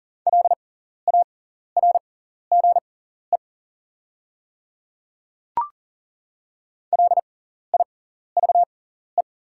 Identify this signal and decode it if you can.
Morse code, heard as a keyed tone, about 40 wpm, spaced as for 14 wpm, 700 Hz. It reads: LARGE LIVE
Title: 6x Character Spacing - Top 200 Words - 40wpm